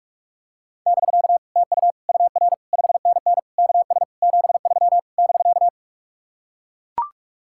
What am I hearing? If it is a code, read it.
Morse code, heard as a keyed tone, about 44 wpm, 700 Hz. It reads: <BK> TU UR 5NN KS 73 <BK>